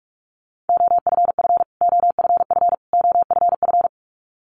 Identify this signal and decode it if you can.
Morse code, heard as a keyed tone, about 48 wpm, 700 Hz. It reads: OFF OFF OFF
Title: Top 500 Words Repeated 45wpm